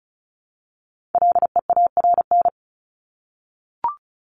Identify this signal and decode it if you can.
Morse code, heard as a keyed tone, about 34 wpm, 700 Hz. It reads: LEARN
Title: Top 300 Words 35wpm